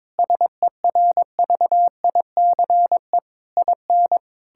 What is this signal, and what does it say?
Morse code, heard as a keyed tone, about 22 wpm, 700 Hz. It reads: SERVICE IN
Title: Sentences from Top 1000 Words - Review Part 6 - 22wpm